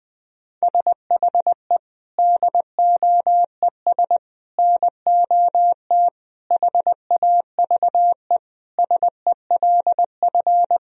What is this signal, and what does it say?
Morse code, heard as a keyed tone, about 20 wpm, 700 Hz. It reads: SHE DOES NOT HAVE SELF
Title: Sentences from Top 1000 Words - Review Part 3 - 20wpm